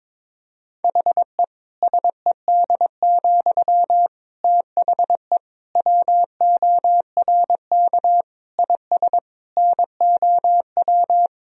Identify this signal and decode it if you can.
Morse code, heard as a keyed tone, about 22 wpm, 700 Hz. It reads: HE SED, THE WORK IS NOW
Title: Sentences from Top 400 Words and Abbreviations 22wpm